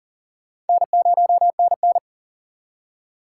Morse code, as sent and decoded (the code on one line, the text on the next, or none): -. ----- -. -.
N0NN